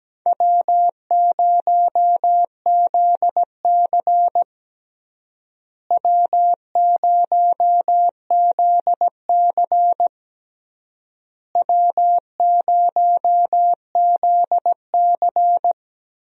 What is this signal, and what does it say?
Morse code, heard as a keyed tone, about 17 wpm, 700 Hz. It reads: W0ZC W0ZC W0ZC
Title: USA Call Signs - 4 Characters - Repeated 3x - 17wpm